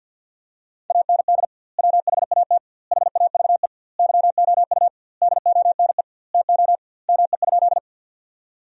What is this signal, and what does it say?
Morse code, heard as a keyed tone, about 50 wpm, 700 Hz. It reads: AND WHAT HAVE YOU DONE TO ME?
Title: 2x Word-Spacing - Sentences from Top 500 Words - 50wpm